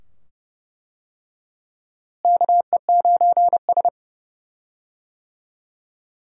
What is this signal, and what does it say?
Morse code, heard as a keyed tone, about 31 wpm, 700 Hz. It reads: KE9S
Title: USA Call Signs - 4 Characters - Repeated 2x - 30wpm